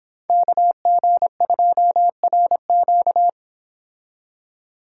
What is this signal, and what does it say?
Morse code, heard as a keyed tone, about 27 wpm, 700 Hz. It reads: KG2RQ
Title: Speed Racing - USA 2x2 Call Signs 22wpm